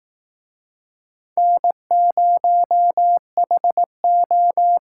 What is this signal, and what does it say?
Morse code, heard as a keyed tone, about 18 wpm, 700 Hz. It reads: N0HO
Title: Speed Racing - USA 1x2 Call Signs 15wpm